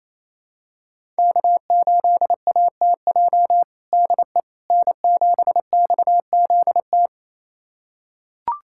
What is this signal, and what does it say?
Morse code, heard as a keyed tone, about 27 wpm, 700 Hz. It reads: K8ATJ DE N7XZT
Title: USA Callsign DE Callsign - 5 Characters - 28wpm